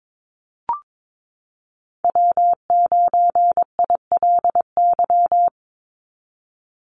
Morse code, heard as a keyed tone, about 22 wpm, 700 Hz. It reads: W9ILY
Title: Call Signs - 22wpm